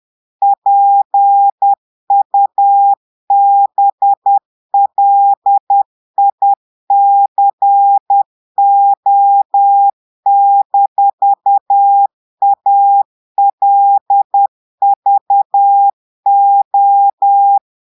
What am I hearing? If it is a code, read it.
Morse code, heard as a keyed tone, about 10 wpm, 800 Hz. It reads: PUBLICO-ALVO
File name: portugais texte en clair 2 vitesse 10 mots minutes